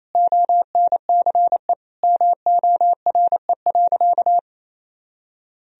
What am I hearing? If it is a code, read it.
Morse code, heard as a keyed tone, about 28 wpm, 700 Hz. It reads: ONCE MORE.
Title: Sentences from Top 500 Words and Abbreviations 28wpm